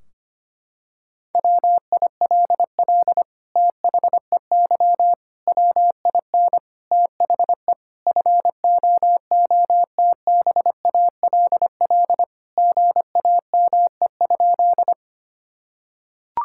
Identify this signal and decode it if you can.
Morse code, heard as a keyed tone, about 25 wpm, 700 Hz. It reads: WILL THEY WIN THE FOOTBALL GAME?
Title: Sentences from Top 1600 Words 25wpm